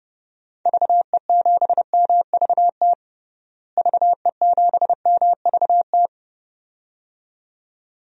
Morse code, heard as a keyed tone, about 30 wpm, 700 Hz. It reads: VE7MVT VE7MVT
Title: DXCC-100 Call Signs - 6 Characters - Repeated 2x - 30wpm